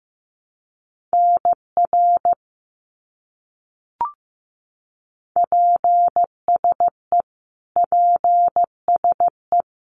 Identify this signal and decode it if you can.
Morse code, heard as a keyed tone, about 15 wpm, 700 Hz. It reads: NR PSE PSE
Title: Abbreviations Repeated 3x - 15wpm